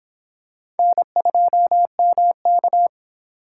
Morse code, heard as a keyed tone, about 25 wpm, 700 Hz. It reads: N2MK